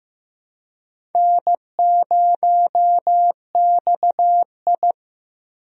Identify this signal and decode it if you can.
Morse code, heard as a keyed tone, about 15 wpm, 700 Hz. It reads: N0XI
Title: USA Call Signs - 4 Characters - Repeated 3x - 15wpm